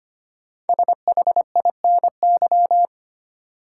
Morse code, heard as a keyed tone, about 25 wpm, 700 Hz. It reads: SHINY
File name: 5 Letter Words - First 3 Letters Given - 25wpm